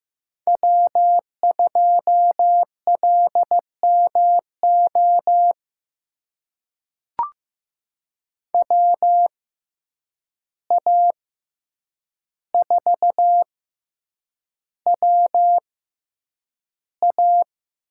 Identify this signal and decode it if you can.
Morse code, heard as a keyed tone, about 15 wpm, spaced as for 8 wpm, 700 Hz. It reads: W2LMO WA4WA
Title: USA Call Signs - 5 Characters - 6x Character Spacing - 15wpm